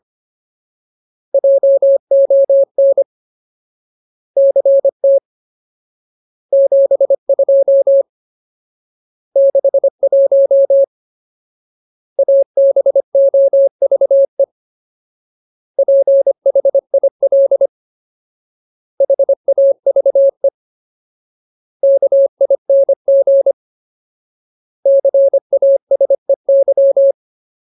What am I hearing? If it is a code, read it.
Morse code, heard as a keyed tone, about 25 wpm, 550 Hz. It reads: JON CT 72 61 ABOVE PHIL HAVE KING CASEY
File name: Ultimate Word List / 25WPM Ultimate